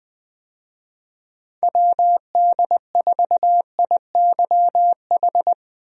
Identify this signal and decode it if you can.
Morse code, heard as a keyed tone, about 20 wpm, 700 Hz. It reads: WD4IYH
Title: USA Call Signs - 6 Characters - 20wpm